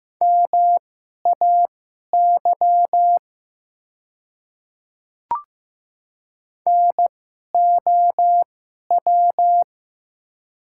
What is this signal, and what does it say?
Morse code, heard as a keyed tone, about 15 wpm, spaced as for 11 wpm, 700 Hz. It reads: MAY NOW
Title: 2x Character Spacing - Top 200 Words - 15wpm